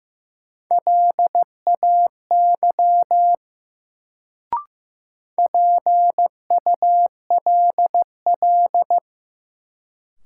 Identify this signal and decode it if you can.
Morse code, heard as a keyed tone, about 15 wpm, 700 Hz. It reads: LAY PULL
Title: Sentences from Top 400 Words - New Word Drill - 15wpm